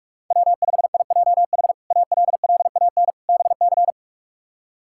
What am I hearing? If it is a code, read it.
Morse code, heard as a keyed tone, about 42 wpm, 700 Hz. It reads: W5IJH ALLAN BC
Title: K1USN SST QSO Practice - 45wpm